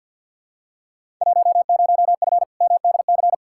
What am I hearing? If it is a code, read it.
Morse code, heard as a keyed tone, about 53 wpm, 700 Hz. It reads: 10F MDC
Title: Field Day Exchanges Encore - 50wpm